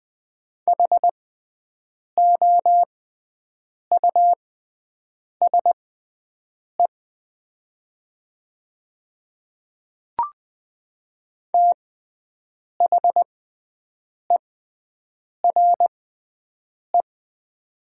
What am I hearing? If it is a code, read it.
Morse code, heard as a keyed tone, about 20 wpm, spaced as for 7 wpm, 690 Hz. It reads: HOUSE THERE
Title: 6x Character Spacing - Top 500 Words - 20wpm